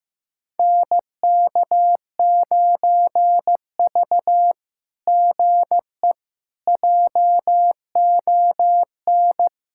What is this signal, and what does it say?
Morse code, heard as a keyed tone, about 16 wpm, 700 Hz. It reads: NK9V GE JON
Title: SOTA QSO Practice 15wpm